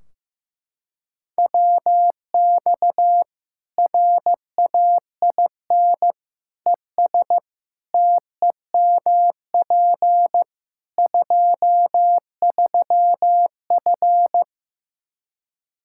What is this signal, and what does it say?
Morse code, heard as a keyed tone, about 15 wpm, 700 Hz. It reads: WX RAIN ES TEMP 23F
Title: K1USN SST QSO Practice 15wpm